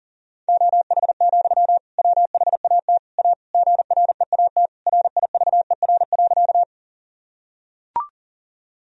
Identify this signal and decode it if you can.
Morse code, heard as a keyed tone, about 40 wpm, 700 Hz. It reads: OH, WHAT A GREAT RIVER.
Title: Sentences from Top 1000 Words - Review Part 1 - 40wpm